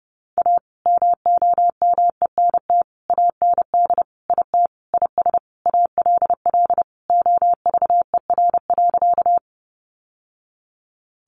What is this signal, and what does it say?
Morse code, heard as a keyed tone, about 30 wpm, 700 Hz. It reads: A MOMENT AND IT IS ALL OVER.
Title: Sentences from Top 1000 Words 30wpm